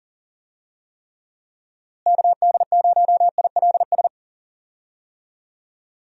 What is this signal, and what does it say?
Morse code, heard as a keyed tone, about 40 wpm, 700 Hz. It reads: KD0ILS
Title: USA Call Signs - 6 Characters - Repeated 3x - 40wpm